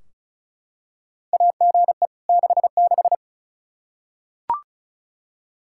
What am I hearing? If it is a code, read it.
Morse code, heard as a keyed tone, about 36 wpm, 700 Hz. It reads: AGE 66
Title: Practice QSO - Exchange 3 - More Info - 35wpm